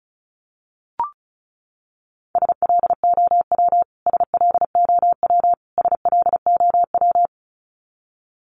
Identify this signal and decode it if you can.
Morse code, heard as a keyed tone, about 34 wpm, 700 Hz. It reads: SLOW SLOW SLOW
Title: Top 500 Words Repeated 35wpm